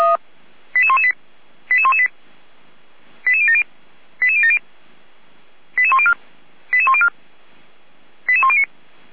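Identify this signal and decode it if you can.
Five-tone selective call (ZVEI), heard as quick runs of five tones, tones of about 70 ms. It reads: 88118 88118 88070 88070 88104 88104 88109
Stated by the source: ZVEI five-tone selective call 88199, received on an SDR, from the Feuerwehr Orlamünde fire brigade audio server.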